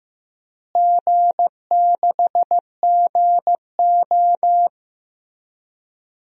Morse code, heard as a keyed tone, about 15 wpm, 700 Hz. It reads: G6GO
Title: DXCC-100 Call Signs - 4 Characters - Repeated 3x - 15wpm